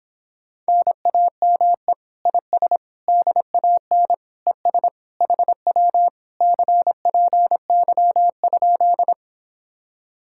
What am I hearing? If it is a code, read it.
Morse code, heard as a keyed tone, about 26 wpm, 700 Hz. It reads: NAME IS DAN ES HW CPY?